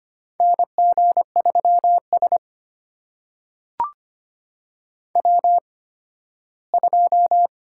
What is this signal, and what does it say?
Morse code, heard as a keyed tone, about 25 wpm, spaced as for 15 wpm, 700 Hz. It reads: NG3S W2
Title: USA Call Signs - 4 Characters - 8x Character Spacing - 25wpm